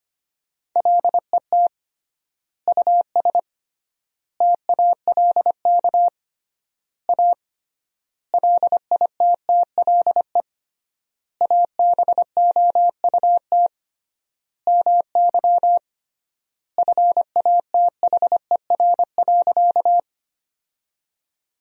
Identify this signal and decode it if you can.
Morse code, heard as a keyed tone, about 25 wpm, 700 Hz. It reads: LET US TALK A LITTLE ABOUT MY FATHER.
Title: 3x Word-Spacing - Sentences from Top 400 Words - 25wpm